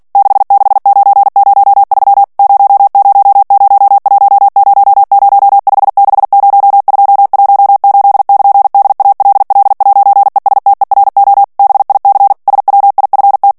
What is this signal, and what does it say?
Morse code, heard as a keyed tone, about 48 wpm, 800 Hz. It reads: 66903 000100560218<KN>DALL'ESTERO BIC SWIFT
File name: morse mp3 / ITALIEN texte clair 1 vitesse 45 mots minute